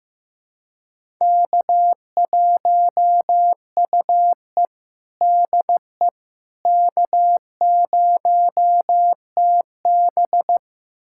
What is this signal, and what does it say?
Morse code, heard as a keyed tone, about 15 wpm, 700 Hz. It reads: K1UE DE K0TB
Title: USA Callsign DE Callsign - 4 Characters - 15wpm